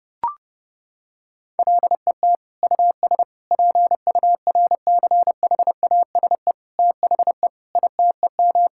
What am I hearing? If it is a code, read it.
Morse code, heard as a keyed tone, about 30 wpm, 700 Hz. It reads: LET US PURCHASE THE ITEM
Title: Sentences from Top 1000 Words - Review Part 6 - 30wpm